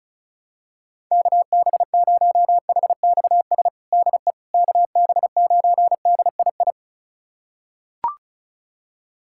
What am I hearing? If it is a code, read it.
Morse code, heard as a keyed tone, about 36 wpm, 700 Hz. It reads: KB0HXS DE KB9DII
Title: USA Callsign DE Callsign - 6 Characters - 35wpm